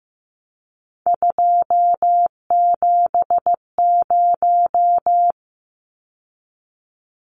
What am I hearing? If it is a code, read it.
Morse code, heard as a keyed tone, about 14 wpm, 700 Hz. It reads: 270
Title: Numbers - 3-Digit 15wpm